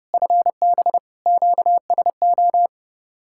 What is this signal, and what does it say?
Morse code, heard as a keyed tone, about 33 wpm, 700 Hz. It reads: FB QSO